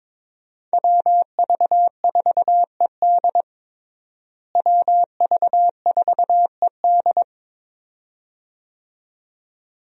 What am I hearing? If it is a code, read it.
Morse code, heard as a keyed tone, about 22 wpm, 700 Hz. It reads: WV4ED WV4ED
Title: USA Call Signs - 5 Characters - Repeated 2x - 22wpm